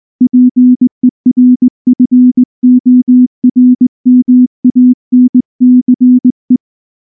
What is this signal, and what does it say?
Morse code, heard as a keyed tone, about 19 wpm, 250 Hz. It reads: PERFORMANCE
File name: morse / morse-training-20WPM-8